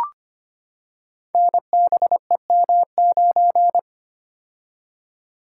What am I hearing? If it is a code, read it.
Morse code, heard as a keyed tone, about 25 wpm, 700 Hz. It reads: NBEM9
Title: Code Groups - 5 Letter:Numbers - 25wpm